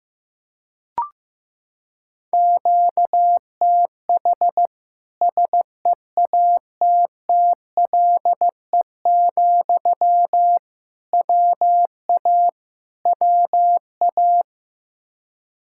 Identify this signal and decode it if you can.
Morse code, heard as a keyed tone, about 15 wpm, 700 Hz. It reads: QTH SEATTLE, WA WA
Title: Practice QSO - Exchange 1 - Beginning - 15wpm